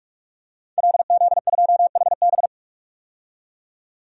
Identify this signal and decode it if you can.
Morse code, heard as a keyed tone, about 42 wpm, 700 Hz. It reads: LZ2HB
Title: DXCC-100 Call Signs - 5 Characters - 45wpm